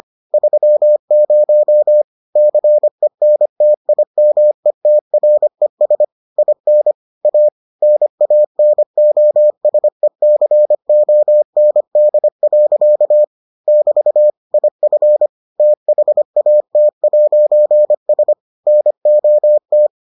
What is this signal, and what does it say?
Morse code, heard as a keyed tone, about 25 wpm, 600 Hz. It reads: 30 CENTIMETRES IN A NANOSECOND. <BT> IF THAT'S NOT